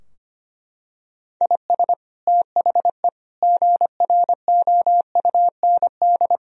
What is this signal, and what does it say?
Morse code, heard as a keyed tone, about 25 wpm, 700 Hz. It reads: IS THE GROUND
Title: Sentences from Top 1000 Words - Review Part 4 - 25wpm